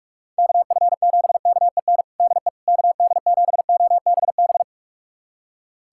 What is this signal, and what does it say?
Morse code, heard as a keyed tone, about 41 wpm, 700 Hz. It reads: KF7KEN DE KD7OBB
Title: USA Callsign DE Callsign - 6 Characters - 45wpm